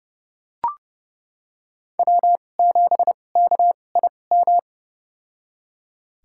Morse code, heard as a keyed tone, about 30 wpm, spaced as for 22 wpm, 700 Hz. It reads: W7KIM
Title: USA Call Signs - 5 Characters - 2x Character Spacing - 30wpm